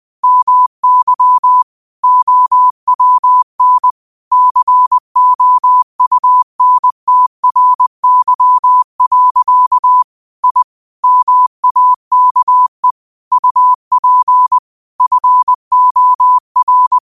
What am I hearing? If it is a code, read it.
Morse code, heard as a keyed tone, about 20 wpm, 1000 Hz. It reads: MY OWN COUNTRY. I MAKE UP FOR